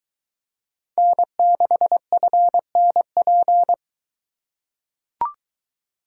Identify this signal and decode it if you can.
Morse code, heard as a keyed tone, about 23 wpm, 700 Hz. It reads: N6FNP